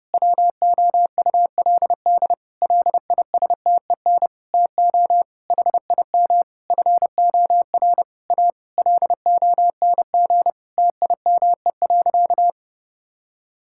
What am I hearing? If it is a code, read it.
Morse code, heard as a keyed tone, about 30 wpm, 700 Hz. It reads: WOULD LISTEN TO HIM FOR A LONG TIME.
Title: Sentences from Top 1600 Words 30wpm